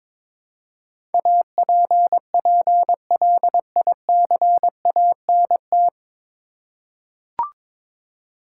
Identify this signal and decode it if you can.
Morse code, heard as a keyed tone, about 22 wpm, 700 Hz. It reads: APPLICANT
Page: Morse Code Ninja practice 22wpm